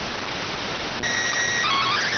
SSTV (slow-scan television), PD 290.